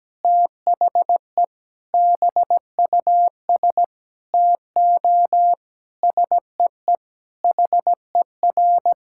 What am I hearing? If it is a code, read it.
Morse code, heard as a keyed tone, about 17 wpm, 700 Hz. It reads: THE BUS TO SEE HER
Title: Sentences from Top 1000 Words - Review Part 6 - 17wpm